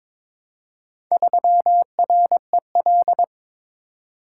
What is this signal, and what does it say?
Morse code, heard as a keyed tone, about 22 wpm, 700 Hz. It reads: 3REL
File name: practice / 22wpm / Four Letter-Number Combinations - 22wpm